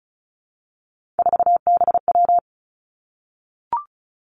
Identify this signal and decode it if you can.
Morse code, heard as a keyed tone, about 36 wpm, 700 Hz. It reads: 4BW
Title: Three Challenging Letter-Number Combinations 35wpm